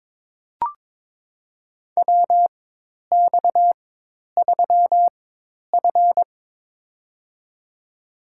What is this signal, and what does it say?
Morse code, heard as a keyed tone, about 22 wpm, spaced as for 10 wpm, 700 Hz. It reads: WX3F
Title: USA Call Signs - 4x Character Spacing - 22wpm